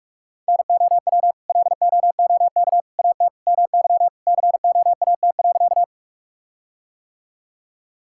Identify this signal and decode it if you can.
Morse code, heard as a keyed tone, about 43 wpm, 700 Hz. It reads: NOW LOOK AT MY COAT.
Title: Sentences from Top 1600 Words 45wpm